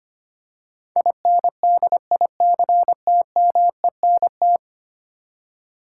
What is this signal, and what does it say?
Morse code, heard as a keyed tone, about 25 wpm, 700 Hz. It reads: INDICTMENT
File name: 10 Letter Words - First 3 Letters Given - 25wpm